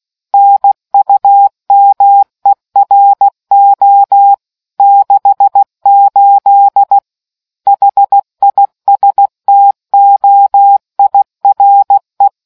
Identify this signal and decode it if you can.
Morse code, heard as a keyed tone, about 16 wpm, 800 Hz. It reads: NUMERO 68 HISTOIRE